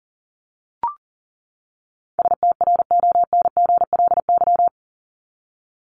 Morse code, heard as a keyed tone, about 40 wpm, 700 Hz. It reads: STRONGLY
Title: LY Suffix - Top 100 Words with Base Words 40wpm